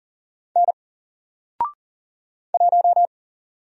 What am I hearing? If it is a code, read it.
Morse code, heard as a keyed tone, about 40 wpm, 700 Hz. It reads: N 1
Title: Single Letter-Number 40wpm